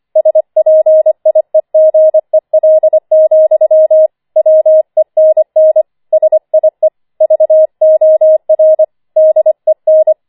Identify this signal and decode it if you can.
Morse code, heard as a keyed tone, about 25 wpm, 630 Hz. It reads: SPIEGEL, WENN SIE VOR DEN
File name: 25wpm